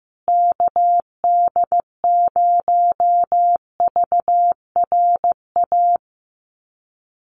Morse code, heard as a keyed tone, about 15 wpm, 700 Hz. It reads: KD0VRA
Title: USA Call Signs Repeated 2x 15wpm